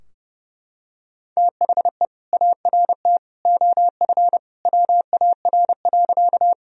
Morse code, heard as a keyed tone, about 32 wpm, 700 Hz. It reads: THE ART OF WAR.
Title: Sentences from Top 1000 Words - Review Part 3 - 30wpm